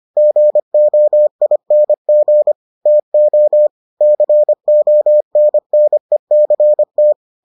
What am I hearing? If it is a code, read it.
Morse code, heard as a keyed tone, about 25 wpm, 600 Hz. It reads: GOING TO CONNECT